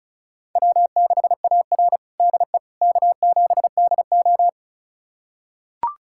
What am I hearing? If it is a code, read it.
Morse code, heard as a keyed tone, about 36 wpm, 690 Hz. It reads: W6AR DE K7DO
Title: USA Callsign DE Callsign - 4 Characters - 35wpm